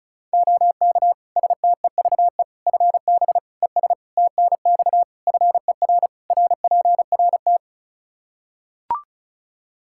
Morse code, heard as a keyed tone, about 34 wpm, 700 Hz. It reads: OK STEVE FB ES TNX FER RPRT
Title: Practice QSO - Exchange 2 - Info - 35wpm